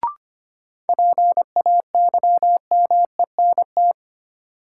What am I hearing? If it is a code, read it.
Morse code, heard as a keyed tone, about 25 wpm, 700 Hz. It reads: PAYMENT
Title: Sentences from Top 1200 Words - New Word Drill Alt - 25wpm